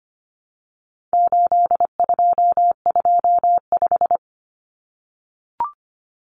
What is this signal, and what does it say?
Morse code, heard as a keyed tone, about 25 wpm, 700 Hz. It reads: 8225
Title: Numbers - 4-Digit 25wpm